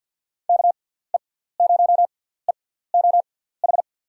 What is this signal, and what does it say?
Morse code, heard as a keyed tone, about 47 wpm, spaced as for 17 wpm, 700 Hz. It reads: KE0EOH
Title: USA Call Signs - 6 Characters - 6x Character Spacing - 50wpm